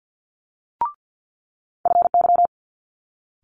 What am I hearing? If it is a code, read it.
Morse code, heard as a keyed tone, about 50 wpm, 700 Hz. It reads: FY
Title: Multi-letter Phonemes Quick Fire 50wpm